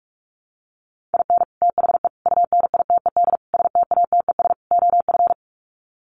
Morse code, heard as a keyed tone, about 44 wpm, 700 Hz. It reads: IN THE UNITED STATES OF
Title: Sets of 5 Words Encore2 45wpm